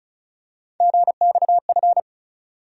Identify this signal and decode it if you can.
Morse code, heard as a keyed tone, about 35 wpm, 700 Hz. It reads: GXF
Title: USA Call Sign Postfixes - 3 Letters 35wpm